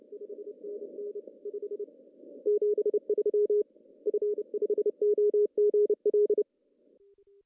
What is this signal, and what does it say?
Morse code, heard as a keyed tone, about 30 wpm, 410 Hz. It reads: 5NN 5 73 F5OGL
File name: CW / cqww-cw-k5tr